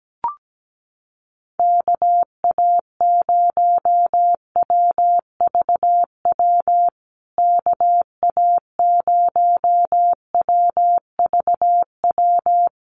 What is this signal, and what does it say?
Morse code, heard as a keyed tone, about 17 wpm, 700 Hz. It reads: KA0WVW KA0WVW
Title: USA Call Signs Repeated 3x 17wpm